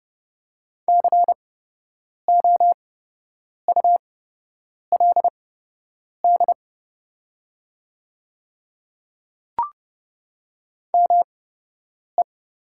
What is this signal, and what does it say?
Morse code, heard as a keyed tone, about 30 wpm, spaced as for 8 wpm, 690 Hz. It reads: COULD ME
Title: 8x Character Spacing - Top 200 Words - 30wpm